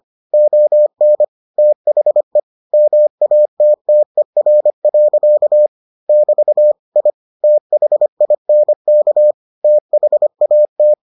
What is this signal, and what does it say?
Morse code, heard as a keyed tone, about 25 wpm, 600 Hz. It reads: ON THE MATTER. <BT> I THINK THAT